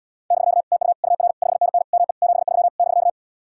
Morse code, heard as a keyed tone, about 80 wpm, 700 Hz. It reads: <BK> TU UR 5NN ME 73 <BK>